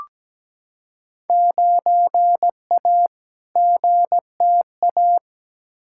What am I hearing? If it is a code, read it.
Morse code, heard as a keyed tone, about 17 wpm, 700 Hz. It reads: 9A GTA
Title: Field Day Exchanges Encore - 17wpm